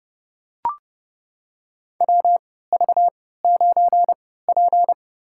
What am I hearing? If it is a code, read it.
Morse code, heard as a keyed tone, about 30 wpm, spaced as for 17 wpm, 700 Hz. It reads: WV9P
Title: USA Call Signs - 4 Characters - 3x Character Spacing - 30wpm